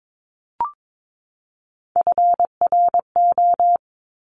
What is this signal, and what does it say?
Morse code, heard as a keyed tone, about 22 wpm, 700 Hz. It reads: FRO
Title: Top Three Letter Combinations in Words 22wpm